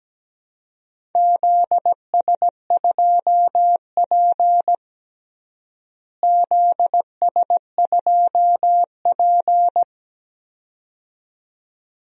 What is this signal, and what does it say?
Morse code, heard as a keyed tone, about 17 wpm, 700 Hz. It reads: ZS2P ZS2P
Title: DXCC-100 Call Signs - 4 Characters - Repeated 2x - 17wpm